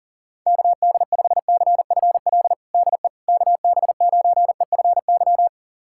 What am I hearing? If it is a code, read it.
Morse code, heard as a keyed tone, about 37 wpm, 700 Hz. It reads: KD5CFL DE KB9EFY
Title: USA Callsign DE Callsign 40wpm